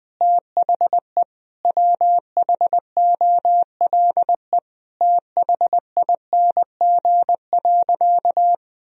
Morse code, heard as a keyed tone, about 20 wpm, 700 Hz. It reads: THE WHOLE THING.
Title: Sentences from Top 1000 Words - Review Part 1 - 20wpm